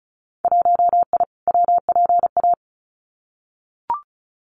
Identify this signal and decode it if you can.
Morse code, heard as a keyed tone, about 34 wpm, 700 Hz. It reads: 1I WPA
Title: Winter Field Day Exchanges Encore 35wpm